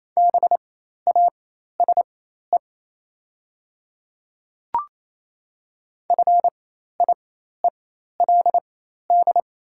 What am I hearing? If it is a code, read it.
Morse code, heard as a keyed tone, about 28 wpm, spaced as for 13 wpm, 700 Hz. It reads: BASE FIELD
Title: 4x Character Spacing - Top 500 Words - 28wpm